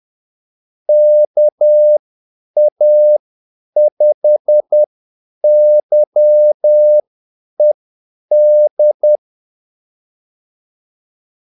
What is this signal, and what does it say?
Morse code, heard as a keyed tone, about 10 wpm, spaced as for 8 wpm, 600 Hz. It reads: KA5YED
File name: Callsigns-10-8wpm